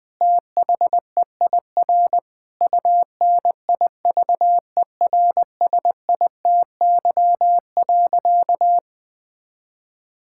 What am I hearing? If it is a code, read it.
Morse code, heard as a keyed tone, about 20 wpm, 700 Hz. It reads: THEIR UNIVERSITY.
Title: Sentences from Top 1000 Words - Review Part 5 - 20wpm